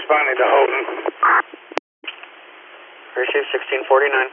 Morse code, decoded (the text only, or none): none